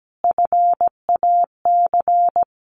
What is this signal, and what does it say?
Morse code, heard as a keyed tone, about 17 wpm, 700 Hz. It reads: FAC
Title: A Sound Of Thunder 17wpm